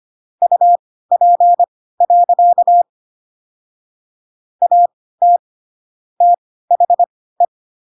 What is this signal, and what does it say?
Morse code, heard as a keyed tone, about 25 wpm, spaced as for 16 wpm, 700 Hz. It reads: UP. AT THE